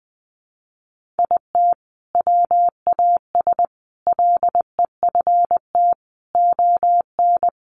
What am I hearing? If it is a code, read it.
Morse code, heard as a keyed tone, about 19 wpm, 700 Hz. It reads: IT WAS LEFT ON
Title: Sentences from Top 700 Words 20wpm